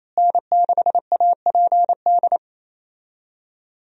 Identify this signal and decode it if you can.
Morse code, heard as a keyed tone, about 28 wpm, 700 Hz. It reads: N6APD
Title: USA Call Signs - 5 Characters - Repeated 2x - 28wpm